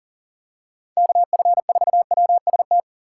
Morse code, heard as a keyed tone, about 40 wpm, 700 Hz. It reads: KF4WST
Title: USA Call Signs - 6 Characters - 2x Character Spacing - 40wpm